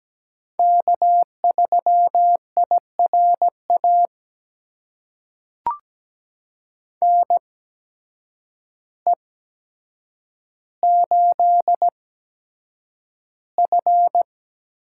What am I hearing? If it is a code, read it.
Morse code, heard as a keyed tone, about 17 wpm, spaced as for 8 wpm, 700 Hz. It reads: K3IRA NE8F
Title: USA Call Signs - 8x Character Spacing - 17wpm